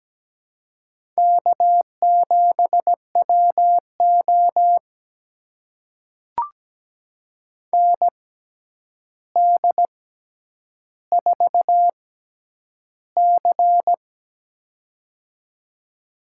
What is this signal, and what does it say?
Morse code, heard as a keyed tone, about 17 wpm, spaced as for 9 wpm, 700 Hz. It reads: K7WO ND4C
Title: USA Call Signs - 4 Characters - 6x Character Spacing - 17wpm